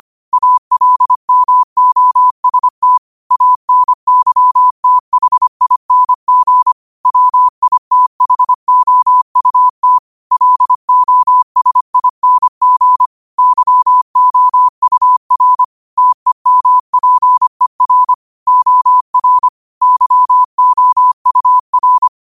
Morse code, heard as a keyed tone, about 25 wpm, 1000 Hz. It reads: ALMOST ANYTHING WITHOUT LOSING YOUR TEMPER OR YOUR
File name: qotdfiles / 25wpm